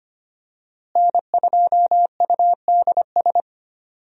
Morse code, heard as a keyed tone, about 25 wpm, 700 Hz. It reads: N2UDS
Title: USA Call Signs - 5 Characters - Repeated 2x - 25wpm